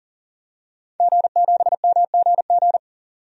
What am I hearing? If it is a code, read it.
Morse code, heard as a keyed tone, about 37 wpm, 700 Hz. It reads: G7MGG
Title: DXCC-100 Call Signs - 5 Characters - Repeated 2x - 40wpm